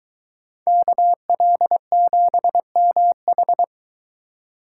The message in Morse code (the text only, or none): KL7MH